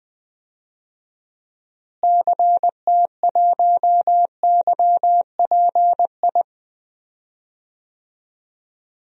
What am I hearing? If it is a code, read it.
Morse code, heard as a keyed tone, about 20 wpm, 700 Hz. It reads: CT1YPI
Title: DXCC-100 Call Signs - 6 Characters - Repeated 3x - 20wpm